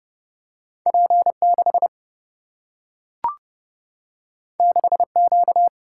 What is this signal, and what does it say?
Morse code, heard as a keyed tone, about 30 wpm, 700 Hz. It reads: P6 6Q
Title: Single-Letter Single-Number 30wpm